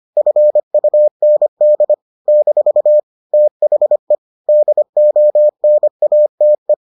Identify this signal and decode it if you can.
Morse code, heard as a keyed tone, about 25 wpm, 600 Hz. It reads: FUND - THE DONATE